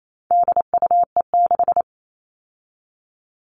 Morse code, heard as a keyed tone, about 28 wpm, 700 Hz. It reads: DUE6
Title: All Cut Numbers - 4-Digit 28wpm